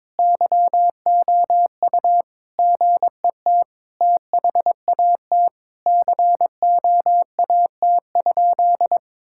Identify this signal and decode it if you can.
Morse code, heard as a keyed tone, about 22 wpm, 700 Hz. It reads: YOU GET THAT COAT?
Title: Sentences from Top 1600 Words 22wpm